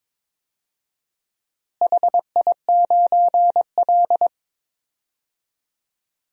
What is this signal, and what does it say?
Morse code, heard as a keyed tone, about 22 wpm, 700 Hz. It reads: HI9L
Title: DXCC-100 Call Signs - 4 Characters - Repeated 2x - 22wpm